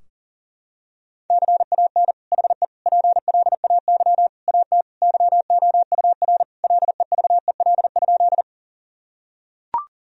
Morse code, heard as a keyed tone, about 37 wpm, 700 Hz. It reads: CAN HE PLAY AT YOUR LEVEL?
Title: Sentences from Top 1000 Words - Review Part 4 - 40wpm